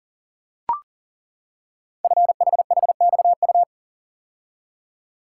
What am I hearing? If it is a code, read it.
Morse code, heard as a keyed tone, about 42 wpm, 690 Hz. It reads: FHHXU
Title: Code Groups - 5 Letters - 40wpm